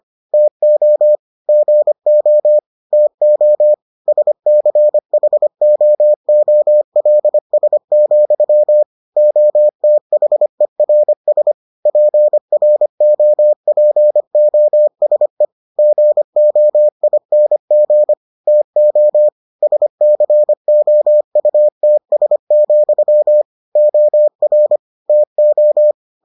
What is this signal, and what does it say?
Morse code, heard as a keyed tone, about 25 wpm, 600 Hz. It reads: TO GO TO SCHOOLS, OTHERS PROPOSE GOING TO SCOUTS, OR TO